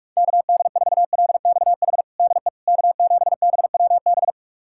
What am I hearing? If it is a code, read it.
Morse code, heard as a keyed tone, about 43 wpm, 700 Hz. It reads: KD4LXH DE K7BWB
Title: USA Callsign DE Callsign 45wpm